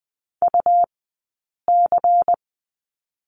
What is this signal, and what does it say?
Morse code, heard as a keyed tone, about 20 wpm, spaced as for 8 wpm, 700 Hz. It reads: UC
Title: 106 Lesson 18d - Letter m - Two Characters 20wpm